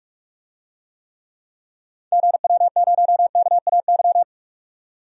Morse code, heard as a keyed tone, about 43 wpm, 700 Hz. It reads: GW0KAY
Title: DXCC-100 Call Signs - 6 Characters - 45wpm